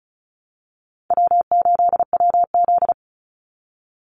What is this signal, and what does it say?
Morse code, heard as a keyed tone, about 34 wpm, 700 Hz. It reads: W8WZ
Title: Call Signs - 35wpm